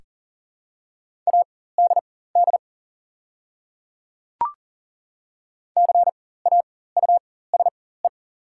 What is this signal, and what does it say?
Morse code, heard as a keyed tone, about 40 wpm, spaced as for 19 wpm, 700 Hz. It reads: ADD CAUSE
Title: 4x Character Spacing - Top 300 Words - 40wpm